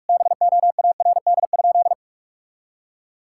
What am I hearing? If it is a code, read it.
Morse code, heard as a keyed tone, about 42 wpm, 700 Hz. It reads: BOARD?
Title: Sentences from Top 1000 Words - Review Part 6 - 45wpm